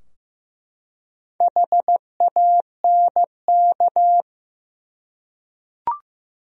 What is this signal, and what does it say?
Morse code, heard as a keyed tone, about 15 wpm, 700 Hz. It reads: HANK
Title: USA Common Names - Max 5 Letters - 15wpm